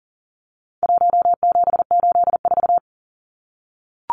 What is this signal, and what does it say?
Morse code, heard as a keyed tone, about 40 wpm, 700 Hz. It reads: 1784
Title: Numbers - 4-Digit 40wpm